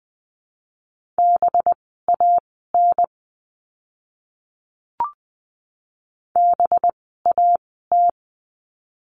Morse code, heard as a keyed tone, about 20 wpm, spaced as for 15 wpm, 700 Hz. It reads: BAN BAT
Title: 185 Lesson 30f - Letter b - Words 20wpm